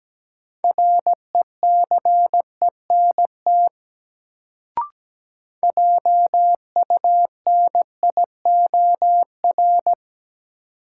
Morse code, heard as a keyed tone, about 17 wpm, 700 Hz. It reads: RECENT JUNIOR
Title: Sentences from Top 1000 Words - Review Part 7 - New Word Drill - 17wpm